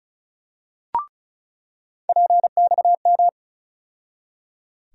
Morse code, heard as a keyed tone, about 34 wpm, 700 Hz. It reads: PXM
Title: USA Call Sign Postfixes - 3 Letters 35wpm